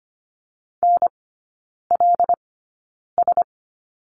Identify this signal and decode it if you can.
Morse code, heard as a keyed tone, about 25 wpm, spaced as for 9 wpm, 690 Hz. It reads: NLS
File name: Lesson 15e - Number 5 - Three Characters 25wpm